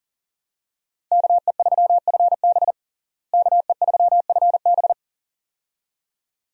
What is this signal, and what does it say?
Morse code, heard as a keyed tone, about 37 wpm, 700 Hz. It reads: KE3FB KE3FB
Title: USA Call Signs - 5 Characters - Repeated 2x - 40wpm